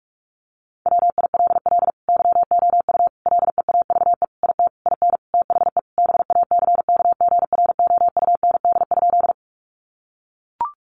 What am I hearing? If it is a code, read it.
Morse code, heard as a keyed tone, about 44 wpm, 700 Hz. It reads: WILL YOU LEAVE IT IN THE BACKGROUND?
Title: Sentences from Top 1300 Words 45wpm